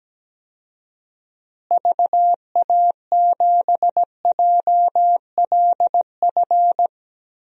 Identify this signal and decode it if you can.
Morse code, heard as a keyed tone, about 17 wpm, 700 Hz. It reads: VA7JLF
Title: DXCC-100 Call Signs - 6 Characters - Repeated 2x - 17wpm